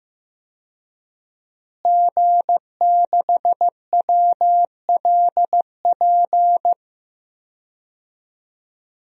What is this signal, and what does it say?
Morse code, heard as a keyed tone, about 15 wpm, 700 Hz. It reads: G6WLP
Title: DXCC-100 Call Signs - 5 Characters - Repeated 2x - 15wpm